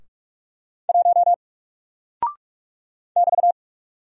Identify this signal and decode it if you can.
Morse code, heard as a keyed tone, about 44 wpm, 700 Hz. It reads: 1 <BT>
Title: Single Character (Letter:Number:Punctuation:Prosign) 45wpm